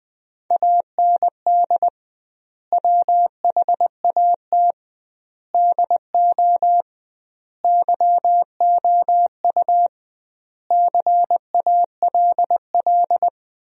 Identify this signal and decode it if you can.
Morse code, heard as a keyed tone, about 21 wpm, 700 Hz. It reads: AND WHAT DO YOU CALL
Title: 2x Word-Spacing - Sentences from Top 100 Words - 20wpm